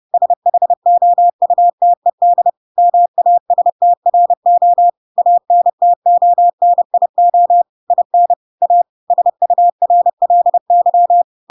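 Morse code, heard as a keyed tone, about 30 wpm, 700 Hz. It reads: SHOUTED MASTRO ANTONIO IN A SURLY
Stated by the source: Chapter 2 of The Adventures of Pinocchio read as a book in Morse code (CW) at 30 wpm, by ik1hge.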